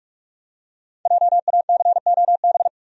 Morse code, heard as a keyed tone, about 46 wpm, 700 Hz. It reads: JACOB
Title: USA Common Names - 45wpm